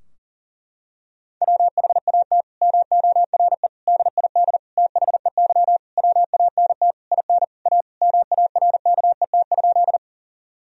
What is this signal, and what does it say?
Morse code, heard as a keyed tone, about 40 wpm, 700 Hz. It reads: WHAT MORE DID THEY WANT IN A MARKET?
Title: Sentences from Top 1000 Words - Review Part 3 - 40wpm